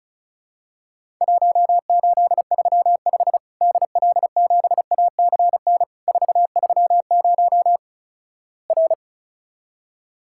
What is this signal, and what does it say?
Morse code, heard as a keyed tone, about 34 wpm, 700 Hz. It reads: 1835 DL7ACN 430 R
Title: Worked All Europe DX Contest - QTC Exchange - 35wpm